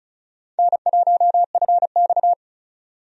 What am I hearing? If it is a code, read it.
Morse code, heard as a keyed tone, about 34 wpm, 700 Hz. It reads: N1FX
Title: USA Call Signs - 4 Characters - 35wpm